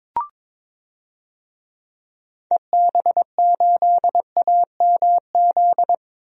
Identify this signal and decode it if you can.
Morse code, heard as a keyed tone, about 22 wpm, 700 Hz. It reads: EB8AMZ
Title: DXCC-100 Call Signs - 6 Characters - 22wpm